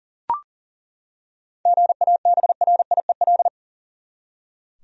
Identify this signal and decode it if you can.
Morse code, heard as a keyed tone, about 40 wpm, 700 Hz. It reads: GABRIEL
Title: USA Common Names - 40wpm